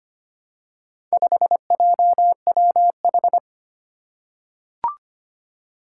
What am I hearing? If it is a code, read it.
Morse code, heard as a keyed tone, about 25 wpm, 700 Hz. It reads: 5JWH
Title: Four Challenging Letter-Number Combinations 25wpm